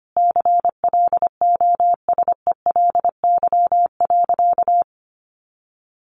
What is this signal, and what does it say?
Morse code, heard as a keyed tone, about 25 wpm, 700 Hz. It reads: CLOSELY.
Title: Sentences from Top 1400 Words 25wpm